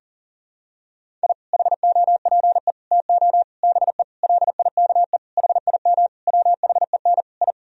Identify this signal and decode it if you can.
Morse code, heard as a keyed tone, about 40 wpm, 700 Hz. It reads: I HOPE TO BE LIKE HIM WHEN I
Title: Sentences from Top 1000 Words - Review Part 1 - 40wpm